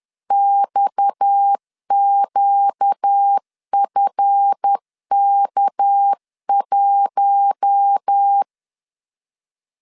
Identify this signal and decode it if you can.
Morse code, heard as a keyed tone, about 10 wpm, 800 Hz. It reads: XQFK1
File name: LEZIONI / 8 YJ45 / 8 Y 10wpm 2615